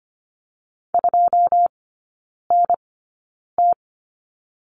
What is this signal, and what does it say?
Morse code, heard as a keyed tone, about 25 wpm, spaced as for 9 wpm, 700 Hz. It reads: 2NT
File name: Lesson 14e - Number 2 - Three Characters 25wpm